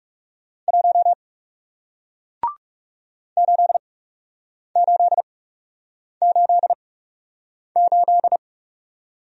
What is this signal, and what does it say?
Morse code, heard as a keyed tone, about 41 wpm, spaced as for 8 wpm, 700 Hz. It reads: 1 8888